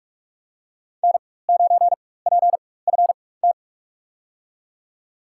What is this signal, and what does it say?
Morse code, heard as a keyed tone, about 46 wpm, spaced as for 21 wpm, 700 Hz. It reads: N9PFT
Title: USA Call Signs - 5 Characters - 4x Character Spacing - 45wpm